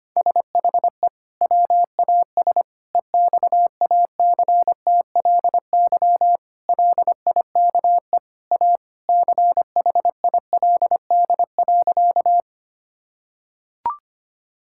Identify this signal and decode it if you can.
Morse code, heard as a keyed tone, about 25 wpm, 700 Hz. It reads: SHE WAS EXACTLY LIKE A CHILD.
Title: Sentences from Top 1000 Words - Review Part 6 - 25wpm